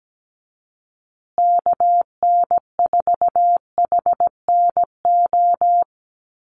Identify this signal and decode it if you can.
Morse code, heard as a keyed tone, about 17 wpm, 700 Hz. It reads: KN4HNO
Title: USA Call Signs Repeated 3x 17wpm